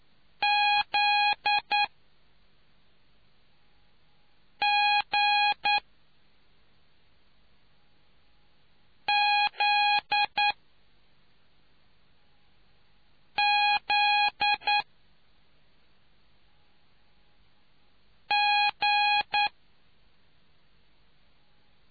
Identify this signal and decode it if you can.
Morse code, heard as a keyed tone, about 9 wpm, 830 Hz. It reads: Z G Z Z G